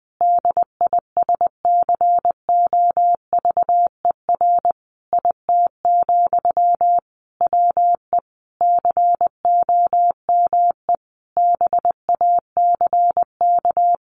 Morse code, heard as a keyed tone, about 20 wpm, 700 Hz. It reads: DISCOVER IT, WE COME BACK
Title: Sentences from Top 1100 Words 20wpm